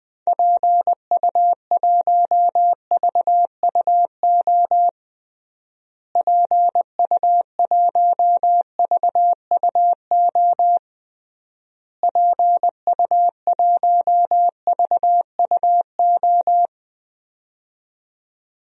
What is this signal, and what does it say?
Morse code, heard as a keyed tone, about 21 wpm, 700 Hz. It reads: PU1VUO PU1VUO PU1VUO
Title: DXCC-100 Call Signs - 6 Characters - Repeated 3x - 20wpm